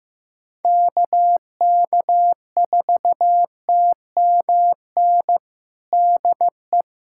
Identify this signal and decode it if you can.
Morse code, heard as a keyed tone, about 15 wpm, 700 Hz. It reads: KK4TMN DE
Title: USA Callsign DE Callsign 15wpm